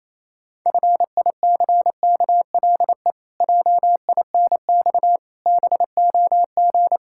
Morse code, heard as a keyed tone, about 28 wpm, 700 Hz. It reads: FICKLE JINX BOG
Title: Pangram Challenge v2 - 28wpm